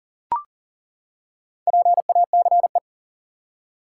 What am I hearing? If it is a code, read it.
Morse code, heard as a keyed tone, about 40 wpm, 700 Hz. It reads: PACE
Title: Four Letter Words - 40wpm